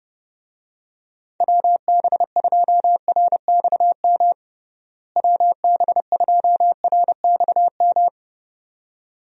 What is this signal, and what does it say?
Morse code, heard as a keyed tone, about 30 wpm, 700 Hz. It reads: WB2RXM WB2RXM
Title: USA Call Signs - 6 Characters - Repeated 2x - 30wpm